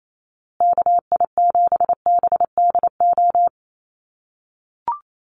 Morse code, heard as a keyed tone, about 28 wpm, 700 Hz. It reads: KI7BDO
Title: USA Call Signs Repeated 2x 28wpm